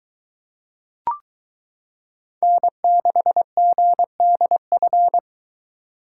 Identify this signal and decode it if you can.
Morse code, heard as a keyed tone, about 23 wpm, 700 Hz. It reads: N6GDF